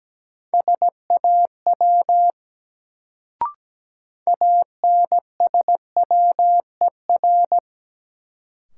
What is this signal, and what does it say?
Morse code, heard as a keyed tone, about 17 wpm, 700 Hz. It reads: SAW ANSWER
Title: Sentences from Top 300 Words - New Word Drill - 17wpm